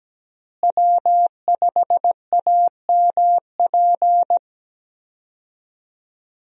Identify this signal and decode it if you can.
Morse code, heard as a keyed tone, about 17 wpm, 700 Hz. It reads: W5AMP
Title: USA Call Signs - 5 Characters - Repeated 2x - 17wpm